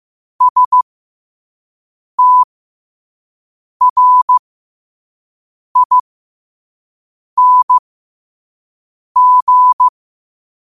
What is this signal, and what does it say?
Morse code, heard as a keyed tone, about 15 wpm, spaced as for 5 wpm, 1000 Hz. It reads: STRING
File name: qotdfiles / 5wpm